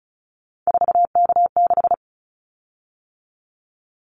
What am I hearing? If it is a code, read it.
Morse code, heard as a keyed tone, about 34 wpm, 700 Hz. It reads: 4K6
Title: Callsign Prefixes for Top 100 DXCC - 35wpm